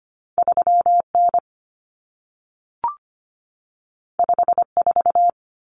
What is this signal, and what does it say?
Morse code, heard as a keyed tone, about 25 wpm, 700 Hz. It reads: 3N 54
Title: Common Cut Numbers - 2-Digit 25wpm